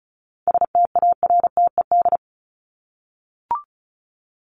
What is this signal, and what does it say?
Morse code, heard as a keyed tone, about 34 wpm, 700 Hz. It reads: STARTED
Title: ED Suffix - Top 100 Words with Base Words 35wpm